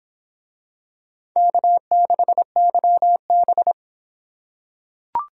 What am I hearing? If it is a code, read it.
Morse code, heard as a keyed tone, about 24 wpm, 700 Hz. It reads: K6YB